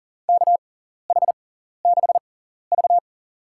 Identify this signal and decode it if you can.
Morse code, heard as a keyed tone, about 40 wpm, spaced as for 14 wpm, 700 Hz. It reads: KH6V
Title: USA Call Signs - 4 Characters - 6x Character Spacing - 40wpm